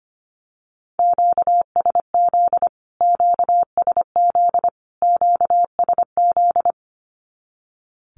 Morse code, heard as a keyed tone, about 25 wpm, 700 Hz. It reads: QSZ QSZ QSZ
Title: Q-Codes Repeated 3x 25wpm